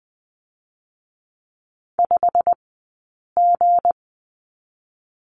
Morse code, heard as a keyed tone, about 20 wpm, 700 Hz. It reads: 5 G
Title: 217 Lesson 35d - Number 0 - Two Characters 20wpm